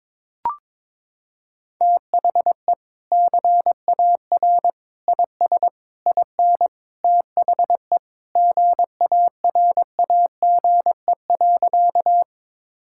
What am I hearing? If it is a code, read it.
Morse code, heard as a keyed tone, about 22 wpm, 700 Hz. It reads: THE CAR IS IN THE GARAGE.
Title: Sentences from Top 1600 Words 22wpm